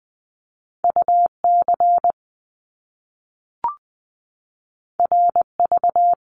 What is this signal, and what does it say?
Morse code, heard as a keyed tone, about 20 wpm, 700 Hz. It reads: UC RV
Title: Top Two Letter Combinations in Words 20wpm